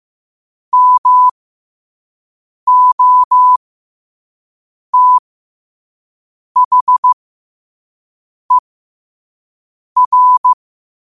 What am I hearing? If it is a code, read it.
Morse code, heard as a keyed tone, about 15 wpm, spaced as for 5 wpm, 1000 Hz. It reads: MOTHER
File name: qotdfiles / 5wpm